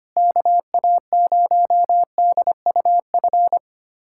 Morse code, heard as a keyed tone, about 25 wpm, 700 Hz. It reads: KA0DUF
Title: USA Callsign DE Callsign 25wpm